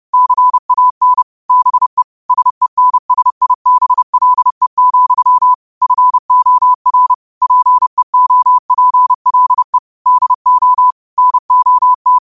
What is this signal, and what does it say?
Morse code, heard as a keyed tone, about 30 wpm, 1000 Hz. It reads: CAN BE SENSIBLE, FOR PEOPLE DO NOT